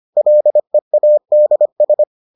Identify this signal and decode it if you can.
Morse code, heard as a keyed tone, about 25 wpm, 600 Hz. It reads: LEADS